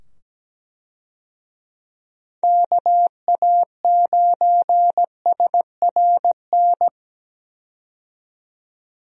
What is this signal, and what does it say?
Morse code, heard as a keyed tone, about 17 wpm, 700 Hz. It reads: KA9SRN
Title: USA Call Signs - 6 Characters - 17wpm